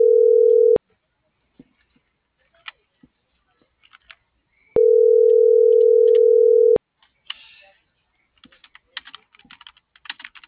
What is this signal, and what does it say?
Ringback tone (North America), 440 + 480 Hz.
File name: CENTRAL MAERSK / out isabel.bravo